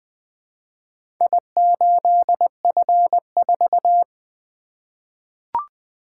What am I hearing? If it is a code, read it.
Morse code, heard as a keyed tone, about 20 wpm, 700 Hz. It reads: I8F4
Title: Four Letter-Number Combinations - 20wpm